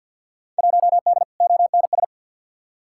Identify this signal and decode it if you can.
Morse code, heard as a keyed tone, about 46 wpm, 700 Hz. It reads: 1D ONS
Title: Field Day Exchanges - 50wpm